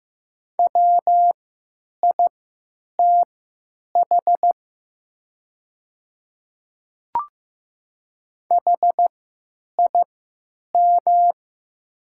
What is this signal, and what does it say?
Morse code, heard as a keyed tone, about 15 wpm, spaced as for 9 wpm, 700 Hz. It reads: WITH HIM
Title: 3x Character Spacing - Top 300 Words - 15wpm